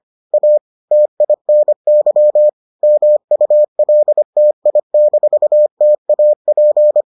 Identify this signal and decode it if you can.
Morse code, heard as a keyed tone, about 25 wpm, 600 Hz. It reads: A TINY MULTI-TAP